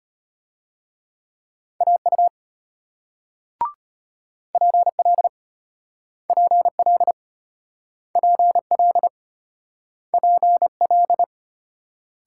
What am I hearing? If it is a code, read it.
Morse code, heard as a keyed tone, about 34 wpm, 700 Hz. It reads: AU PL PL PL PL